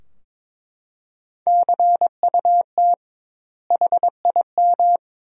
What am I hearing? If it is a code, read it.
Morse code, heard as a keyed tone, about 22 wpm, 700 Hz. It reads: CUT HIM
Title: 2x Word-Spacing - Sentences from Top 300 Words - 22wpm